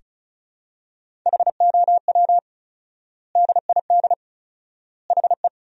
Morse code, heard as a keyed tone, about 34 wpm, 700 Hz. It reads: HOW DID HE
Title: 4x Word-Spacing - Sentences from Top 100 Words - 35wpm